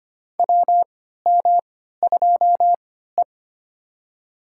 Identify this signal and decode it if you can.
Morse code, heard as a keyed tone, about 25 wpm, spaced as for 14 wpm, 700 Hz. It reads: WM2E
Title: USA Call Signs - 4 Characters - 3x Character Spacing - 25wpm